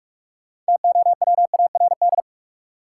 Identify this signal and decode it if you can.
Morse code, heard as a keyed tone, about 43 wpm, 700 Hz. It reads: TOWARD
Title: Sentences from Top 1000 Words - Review Part 6 - New Word Drill - 45wpm